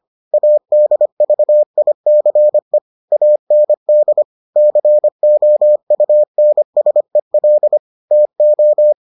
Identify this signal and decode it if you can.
Morse code, heard as a keyed tone, about 25 wpm, 600 Hz. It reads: ADVICE AND COUNSEL TO